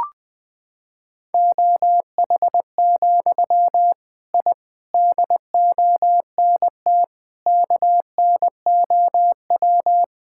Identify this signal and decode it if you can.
Morse code, heard as a keyed tone, about 20 wpm, 700 Hz. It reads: OH, I DONT KNOW
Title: Sentences from Top 1000 Words - Review Part 6 - 20wpm